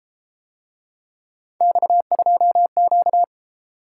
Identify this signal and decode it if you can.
Morse code, heard as a keyed tone, about 33 wpm, 700 Hz. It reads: X2Q